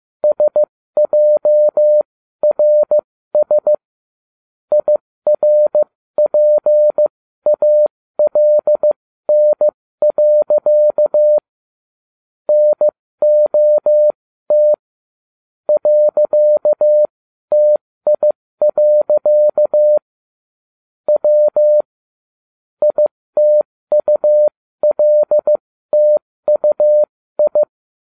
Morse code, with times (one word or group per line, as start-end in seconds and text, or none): 0.2-3.7 SJRS
4.7-11.4 IRPALN.
12.5-14.7 NOT
15.7-20.0 .TI.
21.1-21.8 W
22.8-27.6 ITULTUI